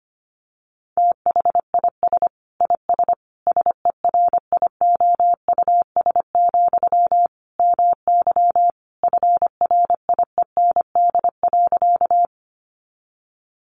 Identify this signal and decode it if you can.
Morse code, heard as a keyed tone, about 25 wpm, 700 Hz. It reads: THIS IS SERIOUS, MY FRIEND.
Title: Sentences from Top 1100 Words 25wpm